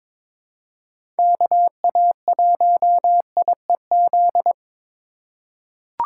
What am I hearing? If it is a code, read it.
Morse code, heard as a keyed tone, about 22 wpm, 700 Hz. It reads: KA1IEZ
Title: USA Call Signs - 6 Characters - 3x Character Spacing - 22wpm